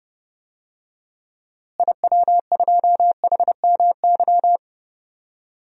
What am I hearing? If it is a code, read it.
Morse code, heard as a keyed tone, about 32 wpm, 700 Hz. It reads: IW2HMY
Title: DXCC-100 Call Signs - 6 Characters - Repeated 2x - 30wpm